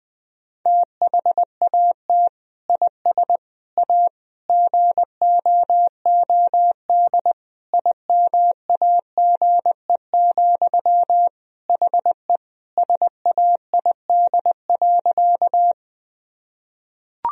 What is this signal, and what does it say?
Morse code, heard as a keyed tone, about 21 wpm, 700 Hz. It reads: THAT IS A GOOD IMAGE, HE SAID.
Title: Sentences from Top 1000 Words - Review Part 4 - 20wpm